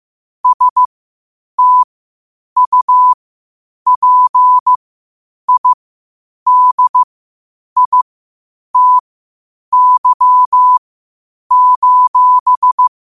Morse code, heard as a keyed tone, about 15 wpm, spaced as for 8 wpm, 1000 Hz. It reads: STUPIDITY:
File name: qotdfiles / 8wpm